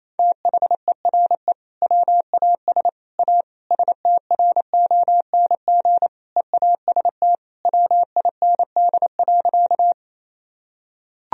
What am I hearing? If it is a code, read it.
Morse code, heard as a keyed tone, about 28 wpm, 700 Hz. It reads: THERE WAS A STRONG EAST WIND.
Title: Sentences from Top 1000 Words - Review Part 3 - 28wpm